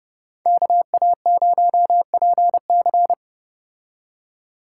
Morse code, heard as a keyed tone, about 34 wpm, 700 Hz. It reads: KA0PC